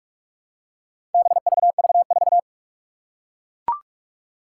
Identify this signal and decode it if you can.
Morse code, heard as a keyed tone, about 44 wpm, 700 Hz. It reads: BVV4
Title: Four Challenging Letter-Number Combinations 45wpm